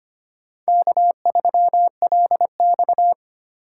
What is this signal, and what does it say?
Morse code, heard as a keyed tone, about 25 wpm, 700 Hz. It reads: K3LX
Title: USA Call Signs - 4 Characters - 3x Character Spacing - 25wpm